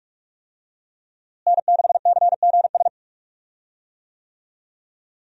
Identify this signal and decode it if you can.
Morse code, heard as a keyed tone, about 46 wpm, 700 Hz. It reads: N6CGS
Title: USA Call Signs - 5 Characters - Repeated 3x - 45wpm